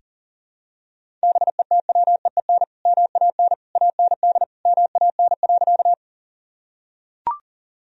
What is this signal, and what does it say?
Morse code, heard as a keyed tone, about 40 wpm, 700 Hz. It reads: BETWEEN MAN AND MAN.
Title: Sentences from Top 1000 Words - Review Part 1 - 40wpm